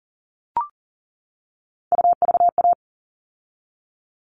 Morse code, heard as a keyed tone, about 40 wpm, 700 Hz. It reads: UVA
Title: All Cut Numbers - 3-Digit 40wpm